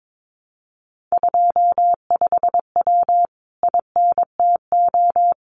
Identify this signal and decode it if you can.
Morse code, heard as a keyed tone, about 22 wpm, 700 Hz. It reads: 25W INTO
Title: Common QSO Phrases Encore 22wpm